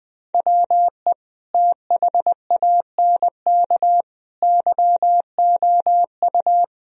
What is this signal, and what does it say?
Morse code, heard as a keyed tone, about 20 wpm, 700 Hz. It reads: WE THANK YOU
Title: Sentences from Top 1000 Words - Review Part 1 - 20wpm